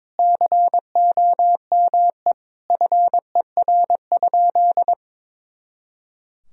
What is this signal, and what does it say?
Morse code, heard as a keyed tone, about 22 wpm, 700 Hz. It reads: COME FER?
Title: Sentences from Top 100 Words and Abbreviations 22wpm